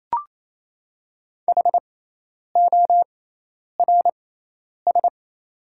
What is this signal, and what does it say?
Morse code, heard as a keyed tone, about 28 wpm, spaced as for 10 wpm, 700 Hz. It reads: HORS
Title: 6x Character Spacing - Top 300 Words - 28wpm